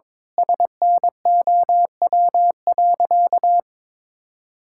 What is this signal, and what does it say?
Morse code, heard as a keyed tone, about 22 wpm, 700 Hz. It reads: SNOW.
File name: practice / 22wpm / Sentences with 8-Word Phrases - 22wpm